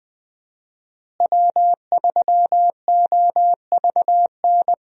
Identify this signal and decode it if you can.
Morse code, heard as a keyed tone, about 20 wpm, 700 Hz. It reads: W3OVN
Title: USA Call Signs - 5 Characters - Repeated 3x - 20wpm